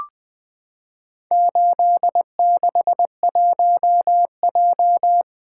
Morse code, heard as a keyed tone, about 20 wpm, 700 Hz. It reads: 861J
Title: Four Letter-Number Combinations - 20wpm